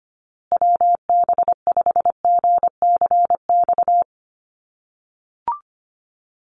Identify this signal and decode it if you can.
Morse code, heard as a keyed tone, about 25 wpm, 700 Hz. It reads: WB5GCX
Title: USA Call Signs Repeated 2x 25wpm